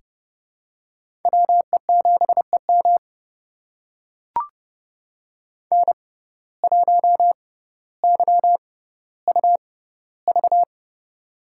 Morse code, heard as a keyed tone, about 30 wpm, spaced as for 15 wpm, 700 Hz. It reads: WE7EM N1YUV
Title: USA Call Signs - 5 Characters - 6x Character Spacing - 30wpm